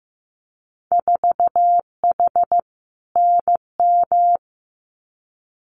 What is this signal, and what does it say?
Morse code, heard as a keyed tone, about 15 wpm, 700 Hz. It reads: 4H NM
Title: Winter Field Day Exchanges Encore 15wpm